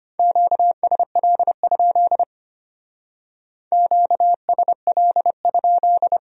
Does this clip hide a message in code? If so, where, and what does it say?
Yes: Morse code from 0.2 to 6.2 s, heard as a keyed tone, about 33 wpm, 700 Hz. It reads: QSL? QSL?